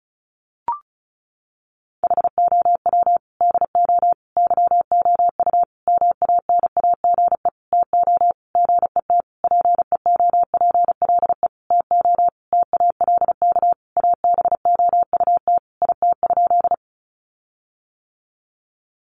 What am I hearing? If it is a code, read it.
Morse code, heard as a keyed tone, about 34 wpm, 700 Hz. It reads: HOW DO YOU MANAGE TO GET PEOPLE TO TALK ABOUT IT?
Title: Sentences from Top 1400 Words 35wpm